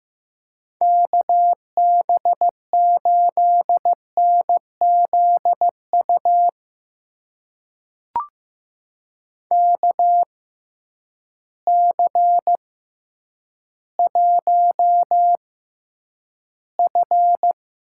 Morse code, heard as a keyed tone, about 15 wpm, spaced as for 9 wpm, 700 Hz. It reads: KB8NZU KC1F
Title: USA Call Signs - 6 Characters - 6x Character Spacing - 15wpm